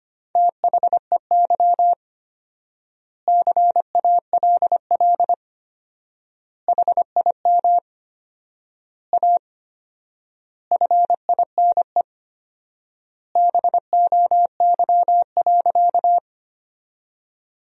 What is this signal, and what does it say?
Morse code, heard as a keyed tone, about 25 wpm, 700 Hz. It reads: THEY CALL HIM A FINE BOY.
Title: 4x Word-Spacing - Sentences from Top 500 Words - 25wpm